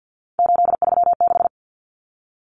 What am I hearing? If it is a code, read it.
Morse code, heard as a keyed tone, about 50 wpm, 700 Hz. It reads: 836
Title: Numbers - 3-Digit 50wpm